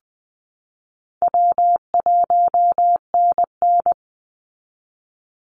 Morse code, heard as a keyed tone, about 20 wpm, 700 Hz. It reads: W1NN
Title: Easy Call Signs Repeated 3x - 20wpm